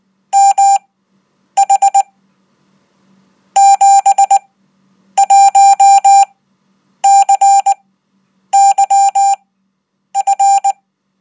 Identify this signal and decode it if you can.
Morse code, heard as a keyed tone, about 19 wpm, spaced as for 8 wpm, 790 Hz. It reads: MH 71CYF
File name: Code